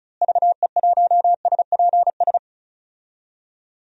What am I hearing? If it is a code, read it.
Morse code, heard as a keyed tone, about 34 wpm, 700 Hz. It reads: VE1SPS
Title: DXCC-100 Call Signs - 6 Characters - Repeated 2x - 35wpm